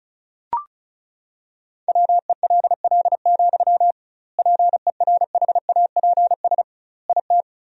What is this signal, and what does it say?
Morse code, heard as a keyed tone, about 36 wpm, 700 Hz. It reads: WELL, PERHAPS IT
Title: 2x Word-Spacing - Sentences from Top 500 Words - 35wpm